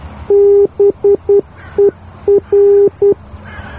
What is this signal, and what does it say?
Morse code, heard as a keyed tone, about 10 wpm, 400 Hz. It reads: BER